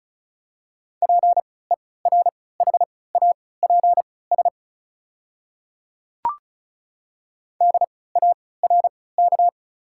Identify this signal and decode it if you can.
Morse code, heard as a keyed tone, about 36 wpm, spaced as for 20 wpm, 700 Hz. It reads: PERHAPS DARK
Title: 3x Character Spacing - Top 500 Words - 35wpm